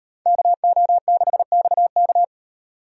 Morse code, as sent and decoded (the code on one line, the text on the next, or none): -.- --- -.... -..- -.-
KO6XK